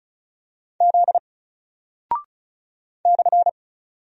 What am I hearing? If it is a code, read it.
Morse code, heard as a keyed tone, about 36 wpm, 700 Hz. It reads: Z /
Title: Single Character (Letter:Number:Punctuation:Prosign) Alt 35wpm